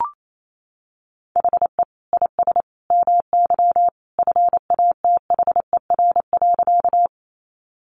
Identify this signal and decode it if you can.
Morse code, heard as a keyed tone, about 28 wpm, 700 Hz. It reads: HE IS MY FATHER.
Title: Sentences from Top 300 Words 28wpm